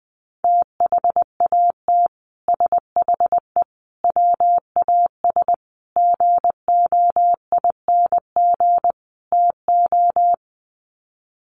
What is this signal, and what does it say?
Morse code, heard as a keyed tone, about 20 wpm, 700 Hz. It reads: THAT SHE WAS GOING TO
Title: Sets of 5 Words Encore2 20wpm